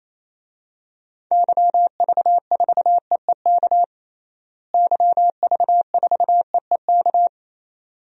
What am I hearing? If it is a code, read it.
Morse code, heard as a keyed tone, about 28 wpm, 700 Hz. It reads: YV4EEK YV4EEK
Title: DXCC-100 Call Signs - 6 Characters - Repeated 2x - 28wpm